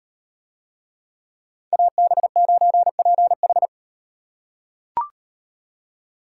Morse code, heard as a keyed tone, about 36 wpm, 700 Hz. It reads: AB9PH